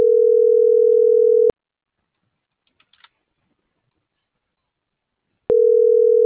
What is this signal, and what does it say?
Ringback tone (North America), 440 + 480 Hz.